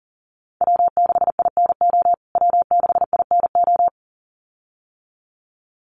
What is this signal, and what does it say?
Morse code, heard as a keyed tone, about 40 wpm, 700 Hz. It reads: W6INO W6INO
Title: USA Call Signs Repeated 2x 40wpm